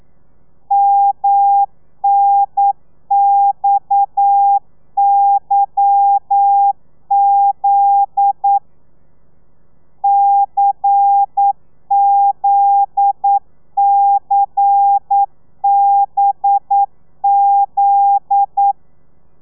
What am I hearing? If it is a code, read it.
Morse code, heard as a keyed tone, about 9 wpm, 800 Hz. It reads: MNXYZ CZCBZ